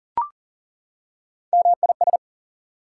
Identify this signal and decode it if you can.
Morse code, heard as a keyed tone, about 40 wpm, 700 Hz. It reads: MIS
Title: Prefixes and Suffixes 40wpm